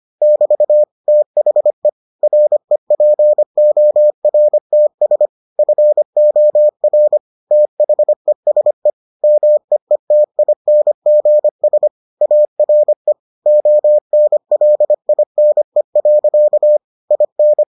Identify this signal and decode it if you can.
Morse code, heard as a keyed tone, about 25 wpm, 600 Hz. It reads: <BT> THE REPORTS FOR THESE MEETINGS ARE ONLINE. IN